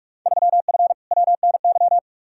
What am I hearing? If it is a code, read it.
Morse code, heard as a keyed tone, about 46 wpm, 700 Hz. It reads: 3F WNY
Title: Field Day Exchanges - 45wpm